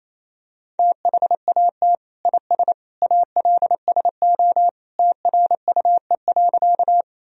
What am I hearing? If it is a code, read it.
Morse code, heard as a keyed tone, about 28 wpm, 700 Hz. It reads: THAT IS ALSO TRUE.
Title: Sentences from Top 1000 Words - Review Part 1 - 28wpm